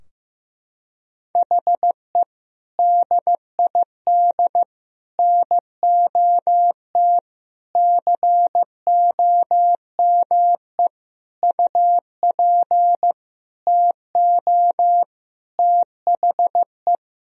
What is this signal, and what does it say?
Morse code, heard as a keyed tone, about 15 wpm, 700 Hz. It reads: HE DID NOT COME UP TO THE
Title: Sentences from Top 1000 Words - Review Part 3 - 15wpm